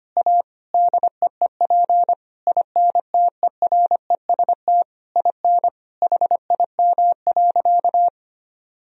Morse code, heard as a keyed tone, about 25 wpm, 700 Hz. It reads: A DEEP INTEREST IN HIM.
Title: Sentences from Top 1000 Words - Review Part 3 - 25wpm